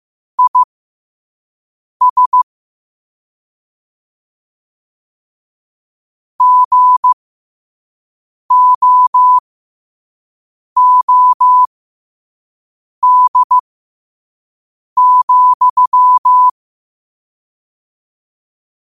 Morse code, heard as a keyed tone, about 15 wpm, spaced as for 5 wpm, 1000 Hz. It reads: IS GOOD,
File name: qotdfiles / 5wpm